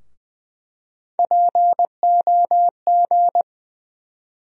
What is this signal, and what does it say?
Morse code, heard as a keyed tone, about 20 wpm, 700 Hz. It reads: POG
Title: USA Call Sign Postfixes - 3 Letters 20wpm